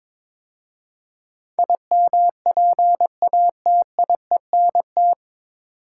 Morse code, heard as a keyed tone, about 22 wpm, 700 Hz. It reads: IMPATIENT
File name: 9 Letter Words - First 3 Letters Given - 22wpm